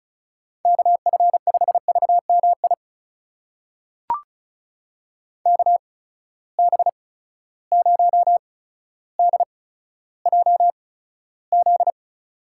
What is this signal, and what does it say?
Morse code, heard as a keyed tone, about 34 wpm, spaced as for 15 wpm, 700 Hz. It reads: KF5VMI KB0DJZ
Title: USA Call Signs - 6 Characters - 8x Character Spacing - 35wpm